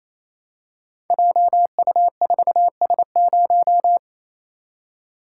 Morse code, heard as a keyed tone, about 27 wpm, 700 Hz. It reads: JU4S0
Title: Code Groups - 5 Letter:Numbers - 28wpm